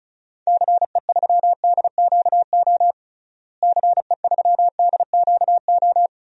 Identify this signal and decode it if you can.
Morse code, heard as a keyed tone, about 36 wpm, 700 Hz. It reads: CE3DQO CE3DQO
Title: DXCC-100 Call Signs - 6 Characters - Repeated 2x - 35wpm